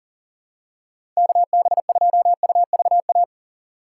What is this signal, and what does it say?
Morse code, heard as a keyed tone, about 40 wpm, 700 Hz. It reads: KB2UVA
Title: USA Call Signs - 8x Character Spacing - 40wpm